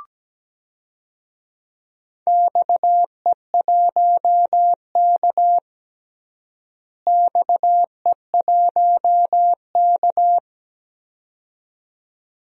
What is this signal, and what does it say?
Morse code, heard as a keyed tone, about 17 wpm, 700 Hz. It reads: XE1K XE1K
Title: DXCC-100 Call Signs - 4 Characters - Repeated 2x - 17wpm